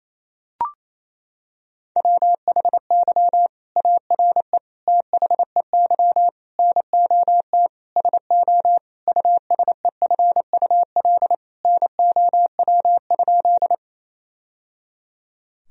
Morse code, heard as a keyed tone, about 28 wpm, 700 Hz. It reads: WHY ARE THEY NOT SO USEFUL NOW?
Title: Sentences from Top 1000 Words - Review Part 4 - 28wpm